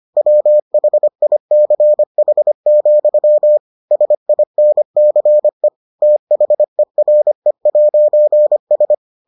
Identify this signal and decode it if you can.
Morse code, heard as a keyed tone, about 25 wpm, 600 Hz. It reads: WHICH, SINCE THERE'S